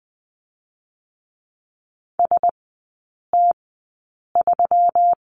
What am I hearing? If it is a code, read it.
Morse code, heard as a keyed tone, about 20 wpm, spaced as for 8 wpm, 700 Hz. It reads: ST3
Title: 138 Lesson 23e - Letter f - Three Characters 20wpm